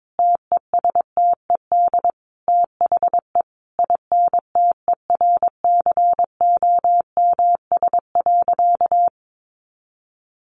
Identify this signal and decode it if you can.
Morse code, heard as a keyed tone, about 22 wpm, 700 Hz. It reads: TESTED THE INTERCOMS.
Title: A Sound Of Thunder 22wpm